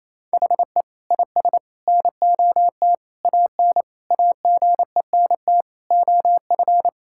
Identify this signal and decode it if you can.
Morse code, heard as a keyed tone, about 28 wpm, 700 Hz. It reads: HE IS NOT AN AGENT OF
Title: Sentences from Top 1600 Words 28wpm